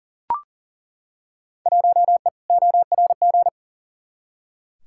Morse code, heard as a keyed tone, about 38 wpm, 700 Hz. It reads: 1E ORG
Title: Field Day Exchanges - 40wpm